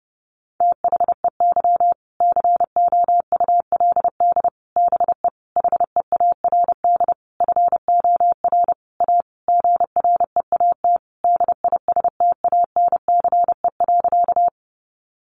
Morse code, heard as a keyed tone, about 28 wpm, 700 Hz. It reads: THEY COULD BE HEARD FOR A GREAT DISTANCE.
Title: Sentences from Top 1200 Words 30wpm